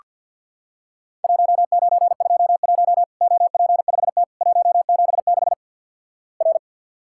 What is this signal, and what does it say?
Morse code, heard as a keyed tone, about 49 wpm, 700 Hz. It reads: 1921 OP5T 176 R
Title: Worked All Europe DX Contest - QTC Exchange - 50wpm